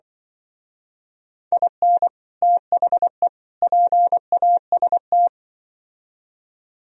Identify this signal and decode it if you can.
Morse code, heard as a keyed tone, about 23 wpm, 700 Hz. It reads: IN THE PAST
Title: Speed-Racing - Sets of 3 Words 20wpm